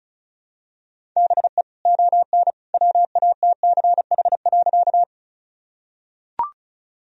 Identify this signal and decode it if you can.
Morse code, heard as a keyed tone, about 34 wpm, 700 Hz. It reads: BE ON WATCH.
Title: Sentences from Top 1000 Words - Review Part 1 - 35wpm